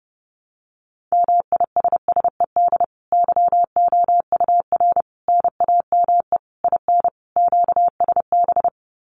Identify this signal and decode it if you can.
Morse code, heard as a keyed tone, about 30 wpm, 700 Hz. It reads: MISSED YOUR NAME IN QSB
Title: Common QSO Phrases 30wpm